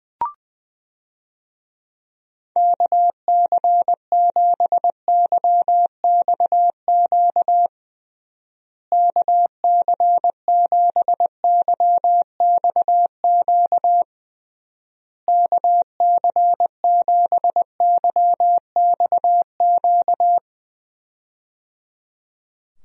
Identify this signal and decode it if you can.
Morse code, heard as a keyed tone, about 21 wpm, 700 Hz. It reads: KC7YXQ KC7YXQ KC7YXQ
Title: USA Call Signs - 6 Characters - Repeated 3x - 20wpm